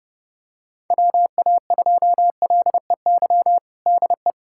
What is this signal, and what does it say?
Morse code, heard as a keyed tone, about 30 wpm, 700 Hz. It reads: WA2LEY DE
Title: USA Callsign DE Callsign 30wpm